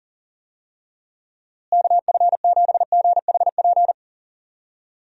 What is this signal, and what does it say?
Morse code, heard as a keyed tone, about 40 wpm, 700 Hz. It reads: KF7GHP
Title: USA Call Signs - 6 Characters - Repeated 2x - 40wpm